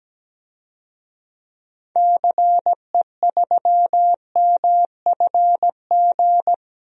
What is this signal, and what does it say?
Morse code, heard as a keyed tone, about 17 wpm, 700 Hz. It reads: CE3MFG
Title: DXCC-100 Call Signs - 6 Characters - 17wpm